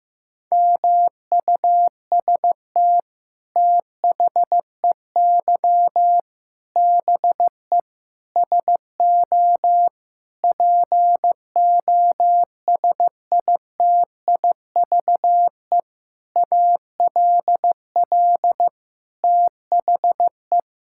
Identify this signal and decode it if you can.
Morse code, heard as a keyed tone, about 15 wpm, 700 Hz. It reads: MUST THEY BE SO POSITIVE ALL THE
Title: Sentences from Top 1000 Words - Review Part 5 - 15wpm